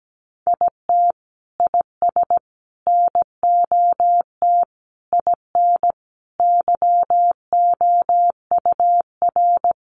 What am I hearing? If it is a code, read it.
Morse code, heard as a keyed tone, about 17 wpm, 700 Hz. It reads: IT IS NOT IN YOUR
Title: Sentences from Top 1200 Words 17wpm